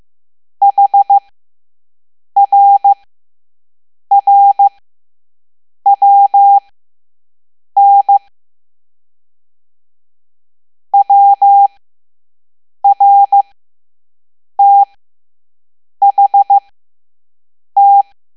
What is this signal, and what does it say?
Morse code, heard as a keyed tone, about 15 wpm, spaced as for 6 wpm, 800 Hz. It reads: HRRWN WRTHT
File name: Morse / Level7-All-A